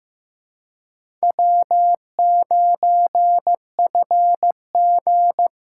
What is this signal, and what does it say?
Morse code, heard as a keyed tone, about 18 wpm, 700 Hz. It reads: W9FG
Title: Speed Racing - USA 1x2 Call Signs 15wpm